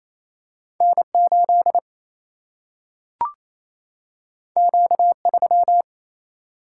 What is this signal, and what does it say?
Morse code, heard as a keyed tone, about 28 wpm, 700 Hz. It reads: N8 Q3
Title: Single-Letter Single-Number 28wpm